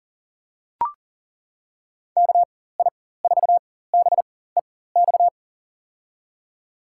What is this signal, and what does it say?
Morse code, heard as a keyed tone, about 40 wpm, spaced as for 19 wpm, 700 Hz. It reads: KI4BEX
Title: USA Call Signs - 6 Characters - 4x Character Spacing - 40wpm